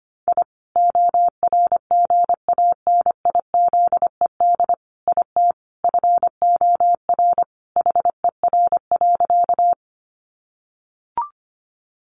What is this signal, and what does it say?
Morse code, heard as a keyed tone, about 25 wpm, 700 Hz. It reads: I ORGANIZED IT FOR HER.
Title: Sentences from Top 1300 Words 25wpm